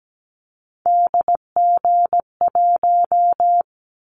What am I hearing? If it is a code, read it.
Morse code, heard as a keyed tone, about 17 wpm, 700 Hz. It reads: DG1
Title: Callsign Prefixes for Top 25 DXCC - 17wpm